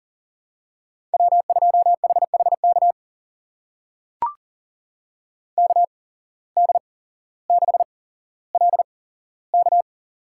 Morse code, heard as a keyed tone, about 39 wpm, spaced as for 17 wpm, 700 Hz. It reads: W2HHK KD6LK
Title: USA Call Signs - 5 Characters - 8x Character Spacing - 40wpm